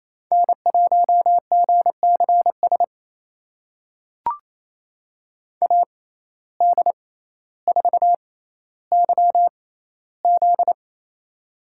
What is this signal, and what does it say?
Morse code, heard as a keyed tone, about 28 wpm, spaced as for 14 wpm, 700 Hz. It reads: N1GCS AD4YZ
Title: USA Call Signs - 5 Characters - 6x Character Spacing - 28wpm